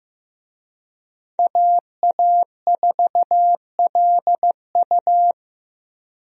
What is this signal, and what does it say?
Morse code, heard as a keyed tone, about 15 wpm, 700 Hz. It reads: AA4LU
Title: USA Call Signs - 5 Characters - Repeated 3x - 15wpm